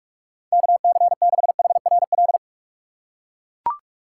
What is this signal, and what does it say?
Morse code, heard as a keyed tone, about 42 wpm, 700 Hz. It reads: KC6HRL
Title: USA Call Signs - 6 Characters - 6x Character Spacing - 45wpm